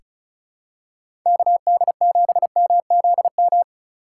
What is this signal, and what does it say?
Morse code, heard as a keyed tone, about 36 wpm, 700 Hz. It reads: KD7MZM
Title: USA Call Signs - 3x Character Spacing - 35wpm